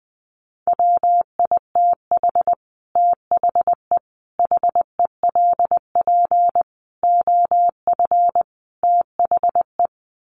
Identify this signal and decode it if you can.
Morse code, heard as a keyed tone, about 20 wpm, 700 Hz. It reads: WITH THE HELP OF THE
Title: Sets of 5 Words Encore 20wpm